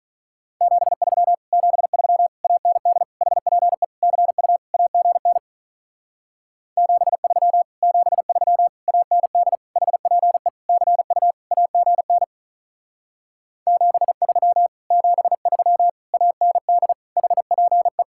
Morse code, heard as a keyed tone, about 44 wpm, 700 Hz. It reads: 73 73 AND HPE CU AGN 73 73 AND HPE CU AGN 73 73 AND HPE